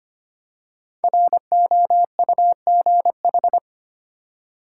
Morse code, heard as a keyed tone, about 25 wpm, 700 Hz. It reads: ROUGH
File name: 5 Letter Words - First 3 Letters Given - 25wpm